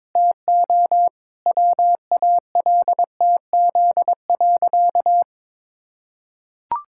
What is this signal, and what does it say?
Morse code, heard as a keyed tone, about 22 wpm, 700 Hz. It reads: TO WALTZ.
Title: Pangram Challenge v2 - 22wpm